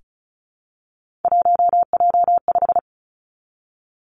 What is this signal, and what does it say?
Morse code, heard as a keyed tone, about 36 wpm, 700 Hz. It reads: 1J5
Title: Three Challenging Letter-Number Combinations 35wpm